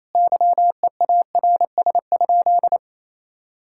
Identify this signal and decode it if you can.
Morse code, heard as a keyed tone, about 28 wpm, 700 Hz. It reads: YEARS?
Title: Sentences from Top 1000 Words - Review Part 6 - 28wpm